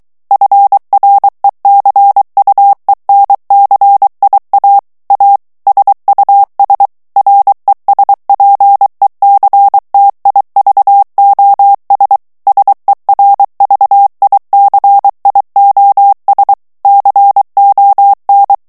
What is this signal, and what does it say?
Morse code, heard as a keyed tone, about 23 wpm, 800 Hz. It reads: FRECUENCIA A SUS RESPECTIVOS SERVICIOS CON
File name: morse mp3 / ESPAGNOL TEXTE EN CLAIR 2 VITESSE 22 MOTS MINUTES